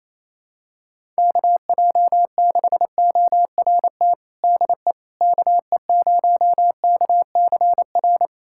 Morse code, heard as a keyed tone, about 27 wpm, 700 Hz. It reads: KJ6ORT DE KE0KCR
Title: USA Callsign DE Callsign - 6 Characters - 28wpm